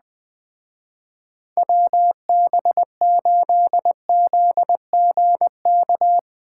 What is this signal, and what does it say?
Morse code, heard as a keyed tone, about 21 wpm, 700 Hz. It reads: WB8ZGK